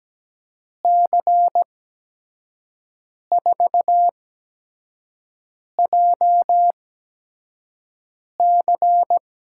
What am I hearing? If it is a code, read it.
Morse code, heard as a keyed tone, about 17 wpm, spaced as for 5 wpm, 700 Hz. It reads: C4JC
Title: USA Call Signs - 5 Characters - 8x Character Spacing - 17wpm